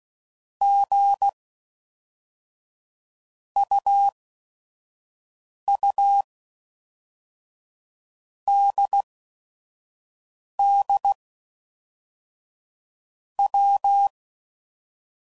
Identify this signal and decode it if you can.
Morse code, heard as a keyed tone, about 16 wpm, 800 Hz. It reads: G U U D D W
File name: Basic Letter Groups By Blocks / Groups@5 / AWUNDG@5 f15